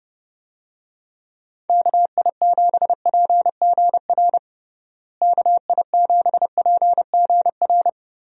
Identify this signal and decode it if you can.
Morse code, heard as a keyed tone, about 30 wpm, 700 Hz. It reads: KI7PGR KI7PGR
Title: USA Call Signs - 6 Characters - Repeated 3x - 30wpm